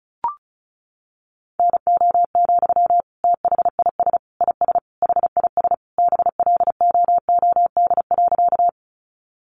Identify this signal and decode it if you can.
Morse code, heard as a keyed tone, about 34 wpm, 700 Hz. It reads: NO, THIS IS HIS BLOOD.
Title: Sentences from Top 1100 Words 35wpm